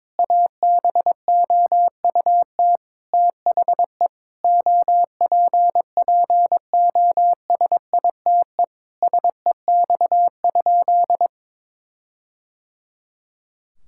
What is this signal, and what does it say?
Morse code, heard as a keyed tone, about 22 wpm, 700 Hz. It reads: ABOUT THE OPPOSITE SEX?
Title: Sentences from Top 1600 Words 22wpm